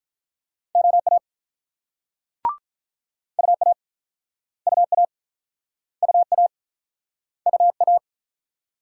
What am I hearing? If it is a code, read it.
Morse code, heard as a keyed tone, about 49 wpm, 700 Hz. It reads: OA UA UA UA UA